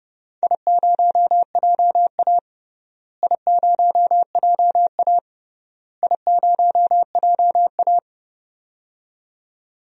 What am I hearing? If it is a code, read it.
Morse code, heard as a keyed tone, about 30 wpm, 700 Hz. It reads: I0JA I0JA I0JA
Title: DXCC-100 Call Signs - 4 Characters - Repeated 3x - 30wpm